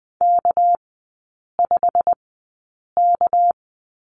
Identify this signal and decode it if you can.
Morse code, heard as a keyed tone, about 20 wpm, spaced as for 8 wpm, 690 Hz. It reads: K5K
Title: 198 Lesson 32e - Letter k - Three Characters 20wpm